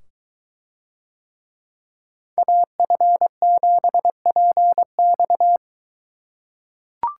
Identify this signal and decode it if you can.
Morse code, heard as a keyed tone, about 22 wpm, 700 Hz. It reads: AF7PX